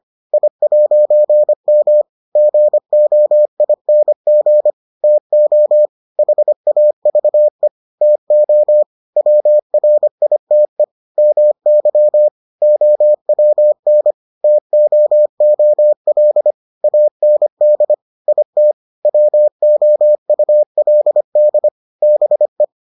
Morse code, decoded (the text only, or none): I'M GOING TO HAVE TO WRITE MY OWN TOOL AND IT WOULD BE